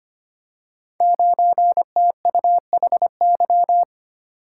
Morse code, heard as a keyed tone, about 25 wpm, 700 Hz. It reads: 9TUHY
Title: Code Groups - 5 Letter:Numbers - 25wpm